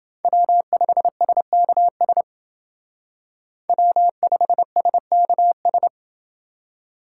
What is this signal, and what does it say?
Morse code, heard as a keyed tone, about 29 wpm, 700 Hz. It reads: W5SKS W5SKS